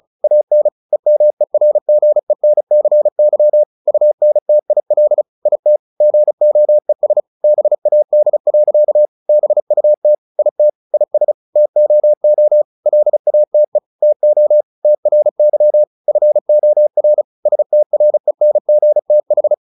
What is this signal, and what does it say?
Morse code, heard as a keyed tone, about 35 wpm, 600 Hz. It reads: AN EMERGENCY UNTIL IT GOES BAD. BUT IT IS TOO LATE TO TRY FOR STRENGTH
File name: the old man and the sea 2 35wpm